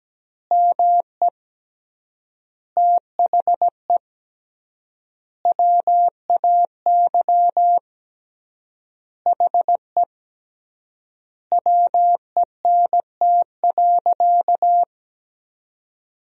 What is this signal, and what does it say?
Morse code, heard as a keyed tone, about 17 wpm, 700 Hz. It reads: ME THE WAY HE WENT.
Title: 3x Word-Spacing - Sentences from Top 200 Words - 17wpm